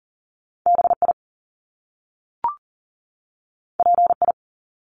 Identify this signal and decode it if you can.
Morse code, heard as a keyed tone, about 40 wpm, 700 Hz. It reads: BI PI
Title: Top Two Letter Combinations in Words 40wpm